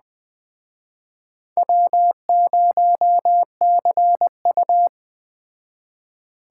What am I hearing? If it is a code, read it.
Morse code, heard as a keyed tone, about 20 wpm, 700 Hz. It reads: W0CU
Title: Speed Racing - USA Call Signs 17wpm